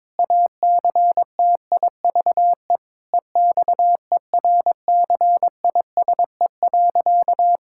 Morse code, heard as a keyed tone, about 22 wpm, 700 Hz. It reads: ACTIVE EXERCISE.
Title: Sentences from Top 1000 Words - Review Part 6 - 22wpm